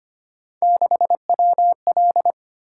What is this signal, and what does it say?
Morse code, heard as a keyed tone, about 25 wpm, 700 Hz. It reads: 6WL
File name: Three Letter-Number Combinations - 25wpm